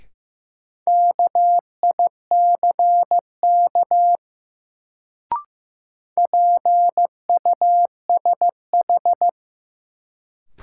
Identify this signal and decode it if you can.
Morse code, heard as a keyed tone, about 15 wpm, 700 Hz. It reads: KICK PUSH
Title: Sentences from Top 1300 Words - New Word Drill Alt - 15wpm